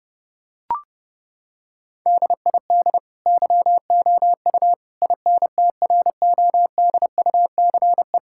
Morse code, heard as a keyed tone, about 30 wpm, 700 Hz. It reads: DID YOU INTRODUCE
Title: Sentences from Top 1600 Words 30wpm